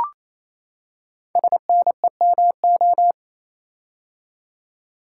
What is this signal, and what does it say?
Morse code, heard as a keyed tone, about 27 wpm, 700 Hz. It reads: SNEMO
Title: Code Groups - 5 Letters - 28wpm